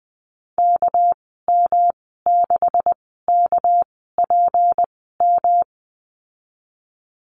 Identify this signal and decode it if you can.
Morse code, heard as a keyed tone, about 19 wpm, spaced as for 14 wpm, 700 Hz. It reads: KM6KPM
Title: 200 Lesson 32g - Letter k - Call Signs 20wpm